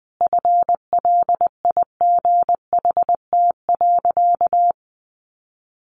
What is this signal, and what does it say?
Morse code, heard as a keyed tone, about 20 wpm, 700 Hz. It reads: FLIGHT.
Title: Sentences from Top 1400 Words 20wpm